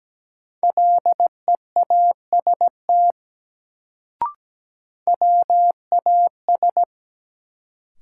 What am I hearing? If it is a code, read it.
Morse code, heard as a keyed tone, about 17 wpm, 700 Hz. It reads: LEAST WAS
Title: Sets of 2 Words - New Word Drill - 17wpm